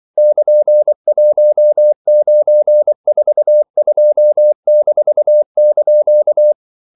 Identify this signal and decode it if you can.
Morse code, heard as a keyed tone, about 24 wpm, 600 Hz. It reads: <KN>1942-)
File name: CW / Practice / EnglishQuotations0224